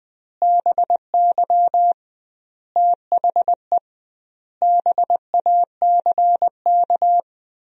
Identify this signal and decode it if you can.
Morse code, heard as a keyed tone, about 20 wpm, 700 Hz. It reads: BY THE BACK
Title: 2x Word-Spacing - Sentences from Top 300 Words - 20wpm